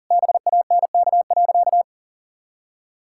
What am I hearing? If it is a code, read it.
Morse code, heard as a keyed tone, about 37 wpm, 700 Hz. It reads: BANK.
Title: Sentences from Top 1000 Words - Review Part 6 - 40wpm